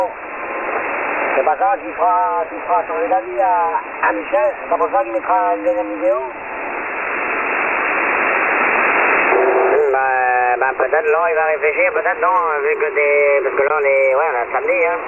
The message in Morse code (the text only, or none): none